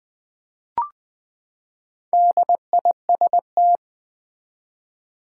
Morse code, heard as a keyed tone, about 20 wpm, 700 Hz. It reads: DIST
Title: Build A Word - 4 Letters At A Time - 8-Letter Words - 20wpm